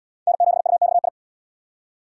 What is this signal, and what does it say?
Morse code, heard as a keyed tone, about 78 wpm, 700 Hz. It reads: N7UXI